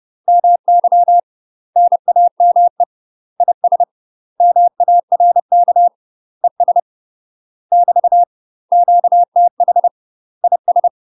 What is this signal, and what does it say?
Morse code, heard as a keyed tone, about 30 wpm, 700 Hz. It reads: MY NAME IS MARK ES <BT> QTH IS
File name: UK MASTER QSOs TRAINING / 30WPM / UK MASTER QSOs 0003